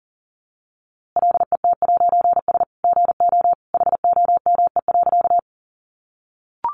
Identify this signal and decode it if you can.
Morse code, heard as a keyed tone, about 40 wpm, 700 Hz. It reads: LET'S GO HOME.
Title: A Sound Of Thunder 40wpm